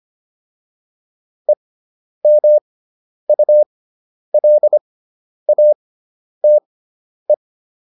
Morse code, heard as a keyed tone, about 25 wpm, spaced as for 10 wpm, 600 Hz. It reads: EMULATE